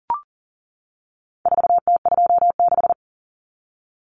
Morse code, heard as a keyed tone, about 40 wpm, 700 Hz. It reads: 4T26
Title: Common Cut Numbers - 4-Digit 40wpm